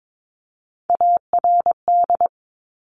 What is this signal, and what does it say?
Morse code, heard as a keyed tone, about 22 wpm, 700 Hz. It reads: ARD
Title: Top Three Letter Combinations in Words 22wpm